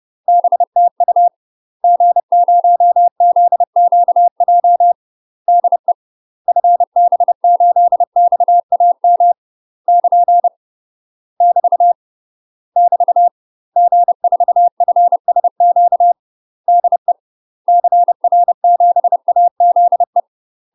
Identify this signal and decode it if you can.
Morse code, heard as a keyed tone, about 30 wpm, 700 Hz. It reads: BTU G0ZQJ DE FB8XAM <KN> <BT> <BT> G4FSQ DE CR7AZE
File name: UK MASTER QSOs TRAINING / 30WPM / UK MASTER QSOs 0019